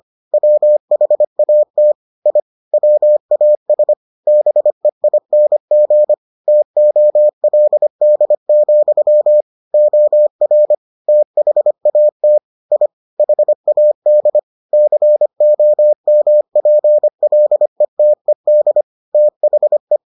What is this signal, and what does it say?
Morse code, heard as a keyed tone, about 25 wpm, 600 Hz. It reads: WHAT I WAS BEING TOLD, OR THAT I HAD COMPLETED THE